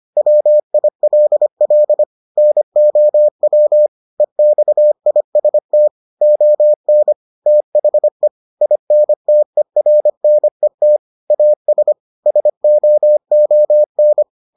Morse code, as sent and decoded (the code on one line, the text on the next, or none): .-- .. .-.. .-.. / -. --- .-- / . -..- .. ... - / --- -. / - .... / .. -. - . .-. -. - / .- ... / ... --- --- -.
WILL NOW EXIST ON THE INTERNET AS SOON